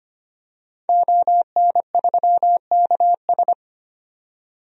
Morse code, heard as a keyed tone, about 25 wpm, 700 Hz. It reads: ON3KS
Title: DXCC-100 Call Signs - 5 Characters - 25wpm